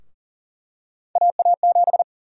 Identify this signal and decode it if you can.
Morse code, heard as a keyed tone, about 40 wpm, 700 Hz. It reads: AA7
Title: USA Call Sign Prefixes - Repeated 2x - 40wpm